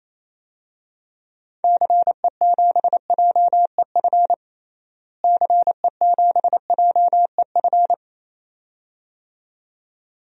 Morse code, heard as a keyed tone, about 28 wpm, 700 Hz. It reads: CE7JEF CE7JEF
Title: DXCC-100 Call Signs - 6 Characters - Repeated 2x - 28wpm